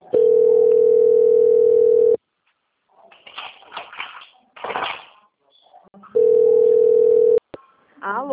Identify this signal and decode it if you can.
Ringback tone (North America), 440 + 480 Hz, with speech.
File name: out- - - - - .3858